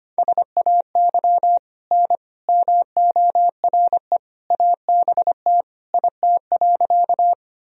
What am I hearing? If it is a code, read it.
Morse code, heard as a keyed tone, about 25 wpm, 700 Hz. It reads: SAY N MORE ABT IT.
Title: Sentences from Top 200 Words and Abbreviations 25wpm